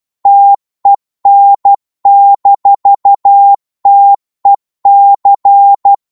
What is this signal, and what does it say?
Morse code, heard as a keyed tone, about 12 wpm, 800 Hz. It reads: TEN-TEC
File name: Morse / qso-12wpm